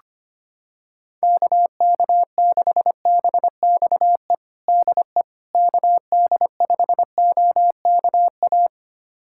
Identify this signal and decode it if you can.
Morse code, heard as a keyed tone, about 25 wpm, 700 Hz. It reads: KK6BXE DE KD5OKA
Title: USA Callsign DE Callsign - 6 Characters - 25wpm